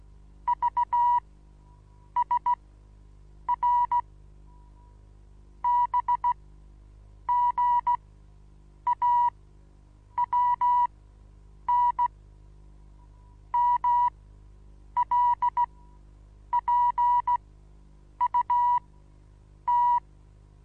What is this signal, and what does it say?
Morse code, heard as a keyed tone, about 16 wpm, spaced as for 8 wpm, 970 Hz. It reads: V S R B G AWN MLPUT
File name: Onlinelehrgang / morsen / DJ4UF Morsen25